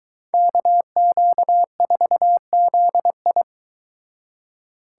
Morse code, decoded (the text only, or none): KQ4ZI